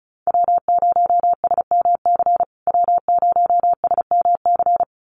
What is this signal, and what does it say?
Morse code, heard as a keyed tone, about 34 wpm, 700 Hz. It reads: W0SMC W0SMC
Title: USA Call Signs Repeated 2x 35wpm